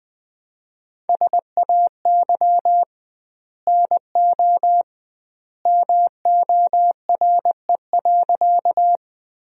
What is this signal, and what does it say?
Morse code, heard as a keyed tone, about 21 wpm, 700 Hz. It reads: SAY NO MORE.
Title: 2x Word-Spacing - Sentences from Top 200 Words - 20wpm